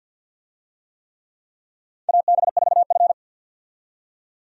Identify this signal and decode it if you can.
Morse code, heard as a keyed tone, about 52 wpm, 700 Hz. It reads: AB4F
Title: USA Call Signs - 4 Characters - 50wpm